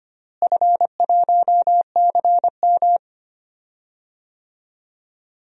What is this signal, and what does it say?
Morse code, heard as a keyed tone, about 25 wpm, 700 Hz. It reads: F1CM
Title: DXCC-100 Call Signs - 4 Characters - Repeated 2x - 25wpm